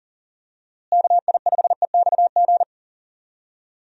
Morse code, heard as a keyed tone, about 37 wpm, 700 Hz. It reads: KI5EXG
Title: USA Call Signs - 6 Characters - Repeated 2x - 40wpm